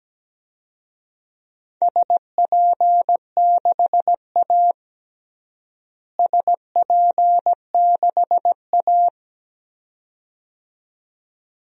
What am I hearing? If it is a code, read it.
Morse code, heard as a keyed tone, about 17 wpm, 700 Hz. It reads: SP6A SP6A
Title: DXCC-100 Call Signs - 4 Characters - Repeated 2x - 17wpm